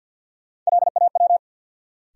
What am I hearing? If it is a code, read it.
Morse code, heard as a keyed tone, about 52 wpm, 700 Hz. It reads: LAW
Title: Sentences from Top 600 Words - New Word Drill - 50wpm